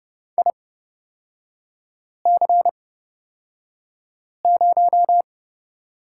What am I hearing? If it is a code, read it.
Morse code, heard as a keyed tone, about 30 wpm, spaced as for 5 wpm, 700 Hz. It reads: IC0
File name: Lesson 36e - BT - Three Characters 30wpm